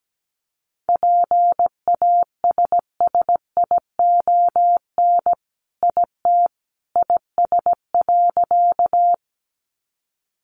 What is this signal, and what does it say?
Morse code, heard as a keyed tone, about 17 wpm, 700 Hz. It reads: PASSION IT IS.
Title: Sentences from Top 1300 Words 17wpm